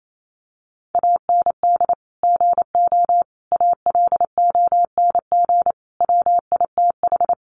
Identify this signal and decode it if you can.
Morse code, heard as a keyed tone, about 28 wpm, 700 Hz. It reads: AND GO ALONG WITH
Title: Sentences from Top 700 Words 28wpm